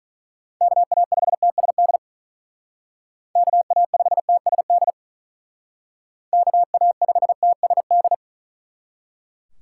Morse code, heard as a keyed tone, about 41 wpm, 700 Hz. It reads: KA5TSD KA5TSD KA5TSD
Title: Speed Racing - USA 2x3 Call Signs 35wpm